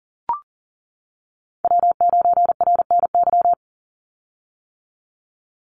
Morse code, heard as a keyed tone, about 41 wpm, 700 Hz. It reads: W9RNY
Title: Call Signs - 40wpm